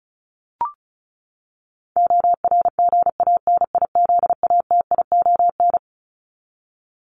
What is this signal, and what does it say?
Morse code, heard as a keyed tone, about 34 wpm, 700 Hz. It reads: ORGANIZATION
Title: Top 501-750 Words 35wpm